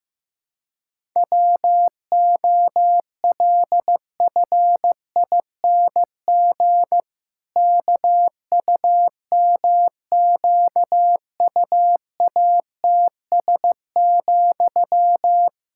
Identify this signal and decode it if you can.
Morse code, heard as a keyed tone, about 16 wpm, 700 Hz. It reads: WOLFING KUMQUATS,
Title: Pangram Challenge v2 - 15wpm